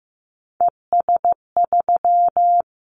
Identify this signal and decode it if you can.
Morse code, heard as a keyed tone, about 15 wpm, 700 Hz. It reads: ES3
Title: Callsign Prefixes for Top 100 DXCC - 15wpm